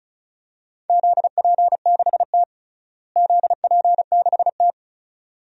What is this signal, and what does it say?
Morse code, heard as a keyed tone, about 36 wpm, 700 Hz. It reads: ZP6T ZP6T
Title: DXCC-100 Call Signs - 4 Characters - Repeated 2x - 35wpm